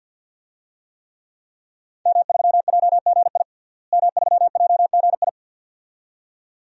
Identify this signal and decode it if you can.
Morse code, heard as a keyed tone, about 49 wpm, 700 Hz. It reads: M3JGI M3JGI
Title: DXCC-100 Call Signs - 5 Characters - Repeated 2x - 50wpm